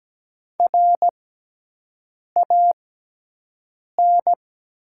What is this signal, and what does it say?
Morse code, heard as a keyed tone, about 17 wpm, spaced as for 6 wpm, 700 Hz. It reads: RAN
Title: 6x Character Spacing - Top 500 Words - 17wpm